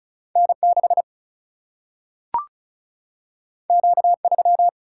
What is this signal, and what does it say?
Morse code, heard as a keyed tone, about 36 wpm, 700 Hz. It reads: N 6 Q 3
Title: Single-Letter Single-Number 35wpm